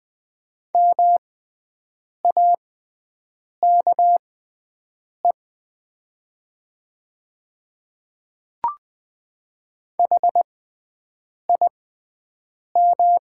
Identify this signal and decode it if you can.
Morse code, heard as a keyed tone, about 20 wpm, spaced as for 7 wpm, 700 Hz. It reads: MAKE HIM
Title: 6x Character Spacing - Top 100 Words - 20wpm